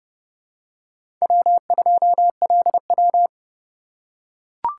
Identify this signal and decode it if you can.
Morse code, heard as a keyed tone, about 30 wpm, 700 Hz. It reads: W2LW
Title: USA Call Signs - 4 Characters - 2x Character Spacing - 30wpm